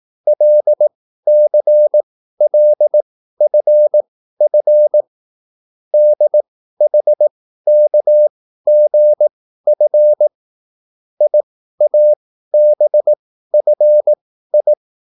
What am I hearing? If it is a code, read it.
Morse code, heard as a keyed tone, about 18 wpm, spaced as for 13 wpm, 600 Hz. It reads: LCLFF DHKGF IABFI
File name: QSL / CW lesson